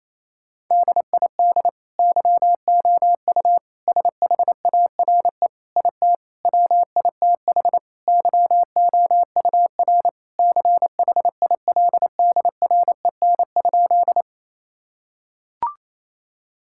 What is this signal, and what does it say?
Morse code, heard as a keyed tone, about 28 wpm, 700 Hz. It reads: DID YOU SHARE IT WITH YOUR CHILDREN?
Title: Sentences from Top 1000 Words - Review Part 6 - 28wpm